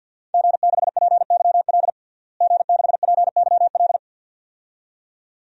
Morse code, heard as a keyed tone, about 50 wpm, 700 Hz. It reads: G6PYL G6PYL
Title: DXCC-100 Call Signs - 5 Characters - Repeated 2x - 50wpm